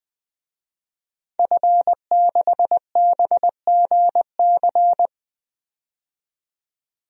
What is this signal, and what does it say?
Morse code, heard as a keyed tone, about 20 wpm, 700 Hz. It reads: F6BGC
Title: DXCC-100 Call Signs - 5 Characters - 20wpm